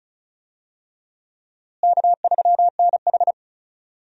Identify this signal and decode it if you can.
Morse code, heard as a keyed tone, about 35 wpm, 700 Hz. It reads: K3NH
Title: USA Call Signs - 4 Characters - Repeated 3x - 35wpm